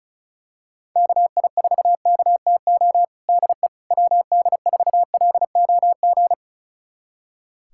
Morse code, heard as a keyed tone, about 34 wpm, 700 Hz. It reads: KI4KTO DE WD4LOG
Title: USA Callsign DE Callsign - 6 Characters - 35wpm